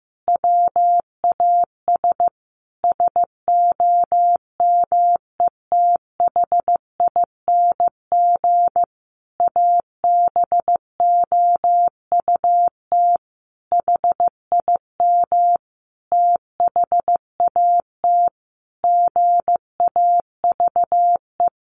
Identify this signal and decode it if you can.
Morse code, heard as a keyed tone, about 15 wpm, 700 Hz. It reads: WAS SOMETHING ABOUT HIM THAT GAVE
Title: Sentences from Top 1500 Words 15wpm